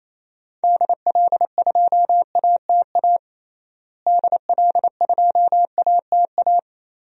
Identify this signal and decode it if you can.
Morse code, heard as a keyed tone, about 28 wpm, 700 Hz. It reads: DL2ATA DL2ATA
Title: DXCC-100 Call Signs - 6 Characters - Repeated 2x - 28wpm